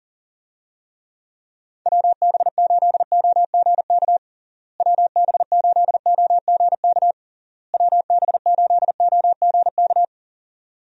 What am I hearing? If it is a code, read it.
Morse code, heard as a keyed tone, about 40 wpm, 700 Hz. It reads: WB8OGK WB8OGK WB8OGK
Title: USA Call Signs - 6 Characters - Repeated 3x - 40wpm